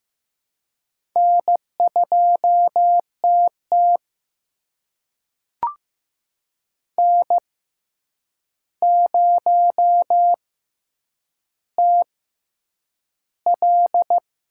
Morse code, heard as a keyed tone, about 15 wpm, spaced as for 8 wpm, 700 Hz. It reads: N2TT N0TL
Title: USA Call Signs - 4 Characters - 6x Character Spacing - 15wpm